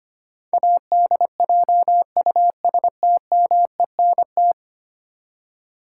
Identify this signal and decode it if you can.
Morse code, heard as a keyed tone, about 25 wpm, 700 Hz. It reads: ADJUSTMENT
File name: 10 Letter Words - First 3 Letters Given - 25wpm